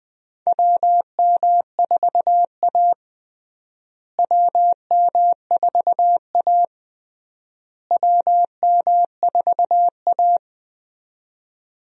Morse code, heard as a keyed tone, about 20 wpm, 700 Hz. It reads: WM4A WM4A WM4A
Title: USA Call Signs - 4 Characters - Repeated 3x - 20wpm